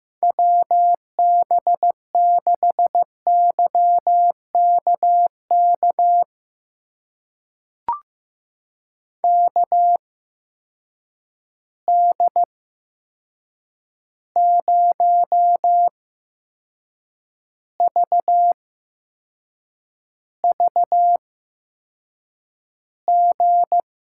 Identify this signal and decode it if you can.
Morse code, heard as a keyed tone, about 15 wpm, spaced as for 6 wpm, 700 Hz. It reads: WB6YKK KD0VVG
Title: USA Call Signs - 6 Characters - 8x Character Spacing - 15wpm